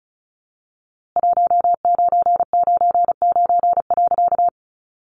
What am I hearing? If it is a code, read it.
Morse code, heard as a keyed tone, about 34 wpm, 700 Hz. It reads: 1999.
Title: A Sound Of Thunder 35wpm